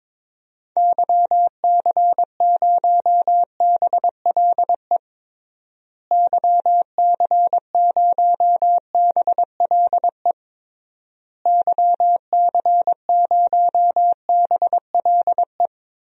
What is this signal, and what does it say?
Morse code, heard as a keyed tone, about 22 wpm, 700 Hz. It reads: YC0BLE YC0BLE YC0BLE
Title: DXCC-100 Call Signs - 6 Characters - Repeated 3x - 22wpm